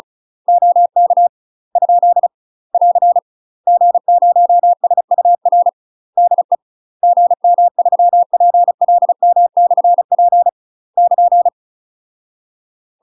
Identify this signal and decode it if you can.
Morse code, heard as a keyed tone, about 34 wpm, 700 Hz. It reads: OK ? <AR> G0SUR DE GM3PLM/P <KN>
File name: UK MASTER QSOs TRAINING / 35WPM / UK MASTER QSOs 0007